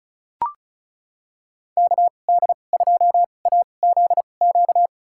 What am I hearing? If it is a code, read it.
Morse code, heard as a keyed tone, about 36 wpm, spaced as for 25 wpm, 700 Hz. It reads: KD2AZQ
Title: USA Call Signs - 6 Characters - 2x Character Spacing - 35wpm